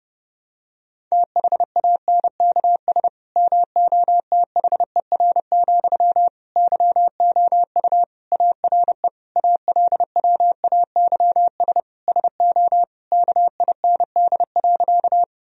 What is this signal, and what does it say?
Morse code, heard as a keyed tone, about 30 wpm, 700 Hz. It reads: THANKS MOTHER, YOU ARE ALWAYS SO KIND.
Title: Sentences from Top 1000 Words - Review Part 3 - 30wpm